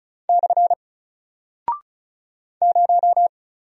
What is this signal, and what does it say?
Morse code, heard as a keyed tone, about 36 wpm, 700 Hz. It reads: / 0
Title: Single Character (Letter:Number:Punctuation:Prosign) Alt 35wpm